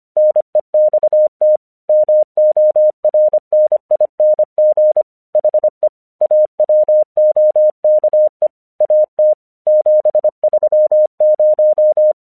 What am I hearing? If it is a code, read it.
Morse code, heard as a keyed tone, about 25 wpm, 610 Hz. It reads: NEXT MORNING HE AWOKE AT 730
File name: English 25wpm